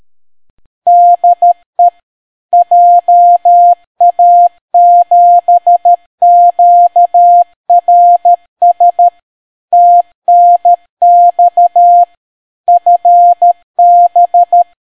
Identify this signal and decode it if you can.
Morse code, heard as a keyed tone, about 13 wpm, 700 Hz. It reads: DE JA7QRS TNX FB